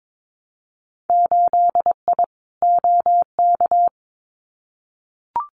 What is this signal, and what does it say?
Morse code, heard as a keyed tone, about 22 wpm, 700 Hz. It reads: 8I OK
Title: Winter Field Day Exchanges Encore 22wpm